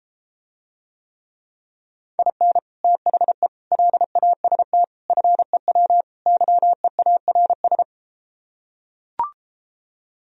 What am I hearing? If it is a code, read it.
Morse code, heard as a keyed tone, about 33 wpm, 700 Hz. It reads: IN THE LAST FEW YEARS